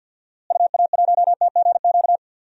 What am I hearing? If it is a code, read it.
Morse code, heard as a keyed tone, about 53 wpm, 700 Hz. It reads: UA1TGQ
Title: DXCC-100 Call Signs - 6 Characters - Repeated 2x - 50wpm